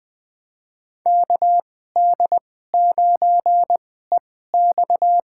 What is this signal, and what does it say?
Morse code, heard as a keyed tone, about 20 wpm, spaced as for 15 wpm, 700 Hz. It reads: KD9EX
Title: USA Call Signs - 2x Character Spacing - 20wpm